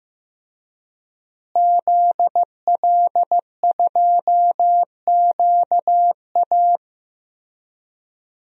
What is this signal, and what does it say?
Morse code, heard as a keyed tone, about 15 wpm, 700 Hz. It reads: ZL2QA
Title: DXCC-100 Call Signs - 5 Characters - Repeated 3x - 15wpm